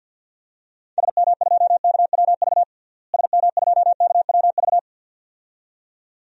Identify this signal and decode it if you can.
Morse code, heard as a keyed tone, about 51 wpm, 700 Hz. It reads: SM2KWV SM2KWV
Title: DXCC-100 Call Signs - 6 Characters - Repeated 2x - 50wpm